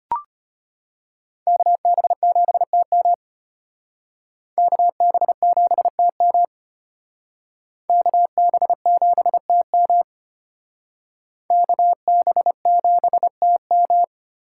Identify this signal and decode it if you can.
Morse code, heard as a keyed tone, about 32 wpm, 700 Hz. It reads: KB7TM KB7TM KB7TM KB7TM